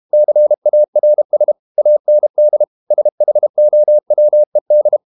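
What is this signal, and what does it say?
Morse code, heard as a keyed tone, about 32 wpm, 600 Hz. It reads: CARS AND SHOWED
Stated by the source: Morse code practice at 32 wpm from CW med Gnister, reading Ken Follett's novel Never in English.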